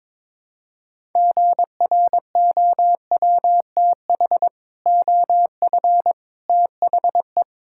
Morse code, heard as a keyed tone, about 22 wpm, 700 Hz. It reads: GROWTH OF THE
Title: Sentences from Top 1000 Words - Review Part 5 - 22wpm